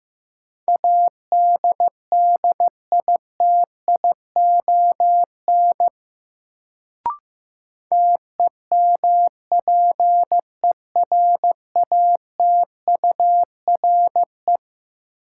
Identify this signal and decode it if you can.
Morse code, heard as a keyed tone, about 15 wpm, 700 Hz. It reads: ADDITION TEMPERATURE
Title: Sentences from Top 1500 Words - New Word Drill Alt - 15wpm